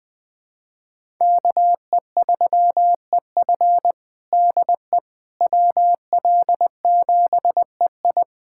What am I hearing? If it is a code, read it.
Morse code, heard as a keyed tone, about 20 wpm, 700 Hz. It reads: KE3EF DE WL7EI
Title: USA Callsign DE Callsign - 5 Characters - 20wpm